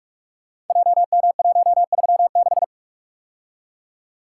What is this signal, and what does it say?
Morse code, heard as a keyed tone, about 44 wpm, 700 Hz. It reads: JM136
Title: Code Groups - 5 Letter:Numbers - 45wpm